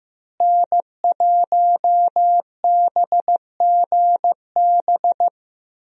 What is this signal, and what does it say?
Morse code, heard as a keyed tone, about 15 wpm, 700 Hz. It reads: N1BGB
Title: USA Call Signs - 5 Characters - Repeated 2x - 15wpm